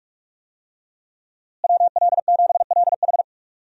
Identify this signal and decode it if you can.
Morse code, heard as a keyed tone, about 44 wpm, 700 Hz. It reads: WL7LH
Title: USA Call Signs - 5 Characters - Repeated 2x - 45wpm